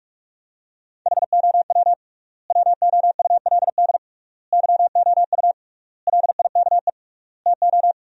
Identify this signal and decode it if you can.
Morse code, heard as a keyed tone, about 46 wpm, 700 Hz. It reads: HOW WOULD YOU LIKE TO
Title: 3x Word-Spacing - Sentences from Top 100 Words - 45wpm